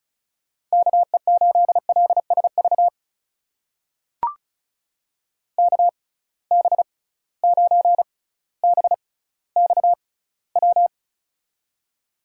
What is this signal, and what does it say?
Morse code, heard as a keyed tone, about 36 wpm, spaced as for 18 wpm, 690 Hz. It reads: KE8LSV KB9BXW
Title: USA Call Signs - 6 Characters - 6x Character Spacing - 35wpm